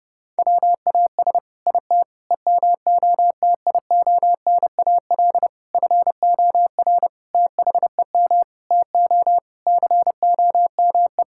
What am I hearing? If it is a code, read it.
Morse code, heard as a keyed tone, about 31 wpm, 700 Hz. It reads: WAS IT EMOTIONAL FOR THEM TO COME
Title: Sentences from Top 1000 Words - Review Part 5 - 30wpm